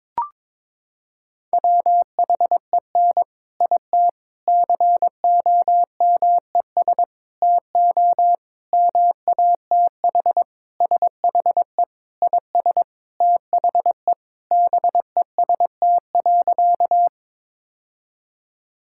Morse code, heard as a keyed tone, about 22 wpm, 700 Hz. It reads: WHEN IT COMES TO MATH SHE IS THE BEST.
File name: practice / 22wpm / Sentences with 3-Word Phrases - 22wpm